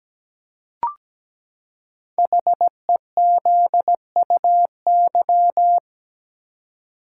Morse code, heard as a keyed tone, about 17 wpm, 700 Hz. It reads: HEZUY
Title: Code Groups - 5 Letters - 17wpm